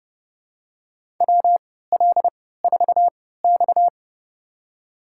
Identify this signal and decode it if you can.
Morse code, heard as a keyed tone, about 30 wpm, spaced as for 17 wpm, 700 Hz. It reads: WL4X
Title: USA Call Signs - 4 Characters - 3x Character Spacing - 30wpm